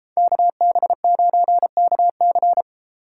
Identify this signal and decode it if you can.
Morse code, heard as a keyed tone, about 32 wpm, 700 Hz. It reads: KB9KC